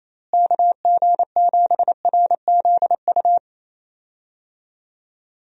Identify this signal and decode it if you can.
Morse code, heard as a keyed tone, about 28 wpm, 700 Hz. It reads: KG7RZU
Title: USA Call Signs - 6 Characters - Repeated 3x - 28wpm